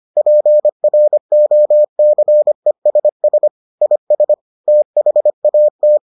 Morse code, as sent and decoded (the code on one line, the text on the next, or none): .--. .-. --- -.-. ... ... / .. ... / - .... .- -
PROCESS IS THAT